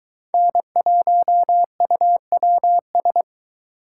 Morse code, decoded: N1UWS

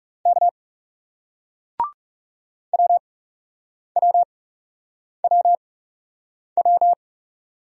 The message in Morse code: K WWWW

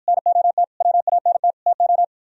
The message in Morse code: NOT WANT TO

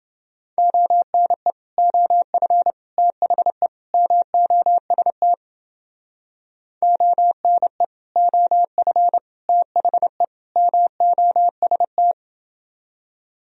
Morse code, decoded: ONE OF THE MOST ONE OF THE MOST

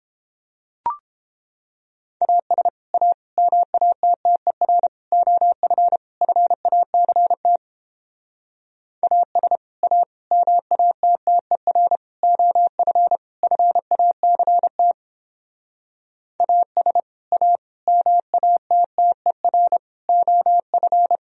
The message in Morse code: AS A MATTER OF FACT AS A MATTER OF FACT AS A MATTER OF

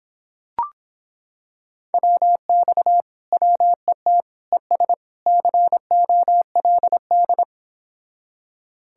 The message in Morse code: WX WET ES COLD

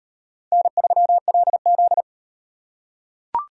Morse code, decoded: N3LZ

